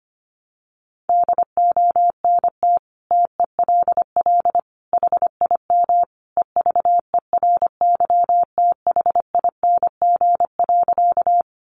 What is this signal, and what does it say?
Morse code, heard as a keyed tone, about 25 wpm, 700 Hz. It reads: DONT TELL HIM EVERYTHING.
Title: Sentences from Top 700 Words 25wpm